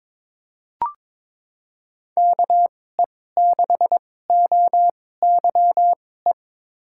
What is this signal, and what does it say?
Morse code, heard as a keyed tone, about 22 wpm, spaced as for 16 wpm, 700 Hz. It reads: KE6OYE
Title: USA Call Signs - 6 Characters - 2x Character Spacing - 22wpm